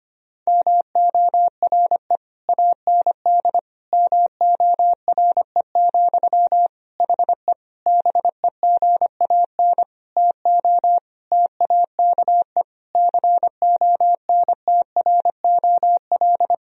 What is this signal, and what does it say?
Morse code, heard as a keyed tone, about 25 wpm, 700 Hz. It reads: MORE AND MORE, HE BEGAN TO TAKE CONTROL
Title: Sentences from Top 1000 Words - Review Part 3 - 25wpm